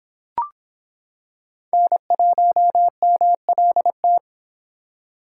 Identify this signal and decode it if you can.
Morse code, heard as a keyed tone, about 25 wpm, 700 Hz. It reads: N1MLT